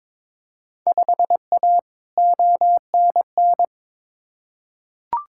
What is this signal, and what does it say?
Morse code, heard as a keyed tone, about 22 wpm, 700 Hz. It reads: 5A ONN
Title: Field Day Exchanges Encore - 22wpm